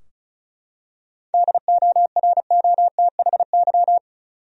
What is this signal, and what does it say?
Morse code, heard as a keyed tone, about 34 wpm, 700 Hz. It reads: DOROTHY
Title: USA Common Names - 35wpm